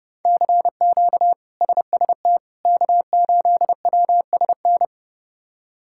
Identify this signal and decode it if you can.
Morse code, heard as a keyed tone, about 30 wpm, 700 Hz. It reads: CQ SST K8WSN
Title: K1USN SST QSO Practice - 30wpm